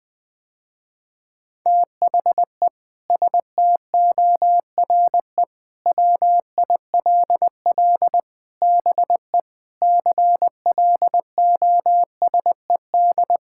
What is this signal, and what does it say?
Morse code, heard as a keyed tone, about 20 wpm, 700 Hz. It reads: THE STORE WILL BE CLOSED
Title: Sentences with 6-Word Phrases - Spoken Beginning - 20wpm